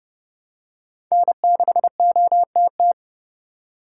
Morse code, heard as a keyed tone, about 30 wpm, 700 Hz. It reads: N6OTT
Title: USA Call Signs - 5 Characters - 8x Character Spacing - 30wpm